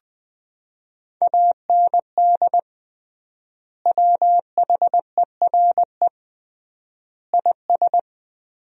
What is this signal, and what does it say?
Morse code, heard as a keyed tone, about 21 wpm, 700 Hz. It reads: AND WHERE IS
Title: 3x Word-Spacing - Sentences from Top 200 Words - 20wpm